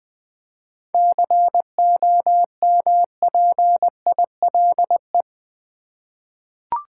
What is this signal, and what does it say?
Morse code, heard as a keyed tone, about 20 wpm, 700 Hz. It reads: COMPILE
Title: 7 Letter Words - First 3 Letters Given - 20wpm